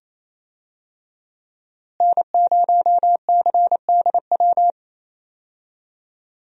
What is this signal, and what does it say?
Morse code, heard as a keyed tone, about 28 wpm, 700 Hz. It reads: N0CDW
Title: USA Call Signs - 5 Characters - Repeated 2x - 28wpm